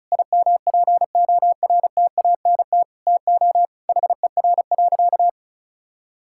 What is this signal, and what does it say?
Morse code, heard as a keyed tone, about 36 wpm, 700 Hz. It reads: IMPORTANT TO HER.
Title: Sentences from Top 1000 Words - Review Part 5 - 35wpm